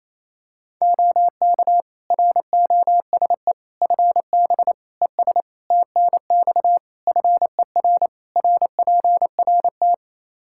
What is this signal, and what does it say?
Morse code, heard as a keyed tone, about 28 wpm, 700 Hz. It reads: OK ROSE FB ES TNX FER RPRT
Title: Practice QSO - Exchange 2 - Info - 28wpm